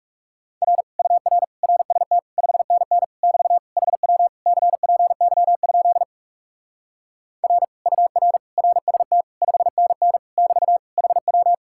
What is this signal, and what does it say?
Morse code, heard as a keyed tone, about 37 wpm, 700 Hz. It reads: R UR RST 5NN <BT> HW CPY? R UR RST 5NN <BT> HW
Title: Speed-Racing Common QSO Phrases Encore - 30wpm